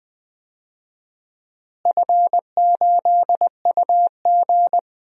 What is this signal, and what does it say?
Morse code, heard as a keyed tone, about 21 wpm, 700 Hz. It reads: F8UG
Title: DXCC-100 Call Signs - 4 Characters - Repeated 2x - 20wpm